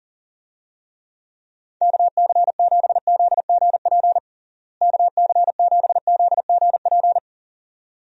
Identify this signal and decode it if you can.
Morse code, heard as a keyed tone, about 40 wpm, 700 Hz. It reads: KC7ZGP KC7ZGP
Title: USA Call Signs - 6 Characters - Repeated 2x - 40wpm